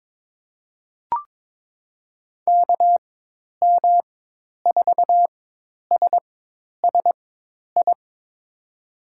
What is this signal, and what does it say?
Morse code, heard as a keyed tone, about 22 wpm, spaced as for 10 wpm, 700 Hz. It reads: KM4SSI
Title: USA Call Signs - 6 Characters - 4x Character Spacing - 22wpm